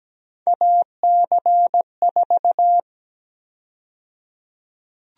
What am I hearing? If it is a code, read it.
Morse code, heard as a keyed tone, about 17 wpm, 700 Hz. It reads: AC4
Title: USA Call Sign Prefixes - 17wpm